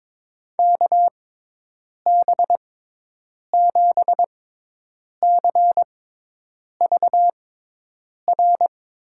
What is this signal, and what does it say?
Morse code, heard as a keyed tone, about 23 wpm, spaced as for 8 wpm, 700 Hz. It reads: KB7CVR
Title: USA Call Signs - 6 Characters - 6x Character Spacing - 22wpm